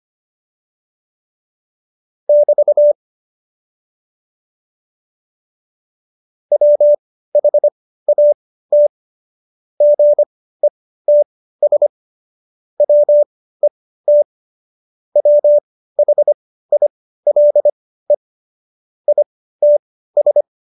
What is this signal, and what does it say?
Morse code, heard as a keyed tone, about 25 wpm, spaced as for 15 wpm, 590 Hz. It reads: <BT> WHAT GETS WET WHILE ITS
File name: Jokes-25-char-15-eff